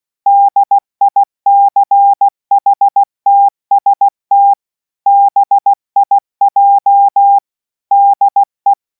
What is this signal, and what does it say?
Morse code, heard as a keyed tone, about 16 wpm, 800 Hz. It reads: DICHTST BIJ DE